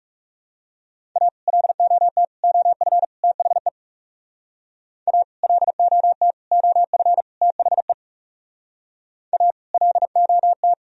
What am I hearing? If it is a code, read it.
Morse code, heard as a keyed tone, about 37 wpm, 700 Hz. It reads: A LOT OF THE A LOT OF THE A LOT
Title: Speed-Racing - Sets of 4 Words 30wpm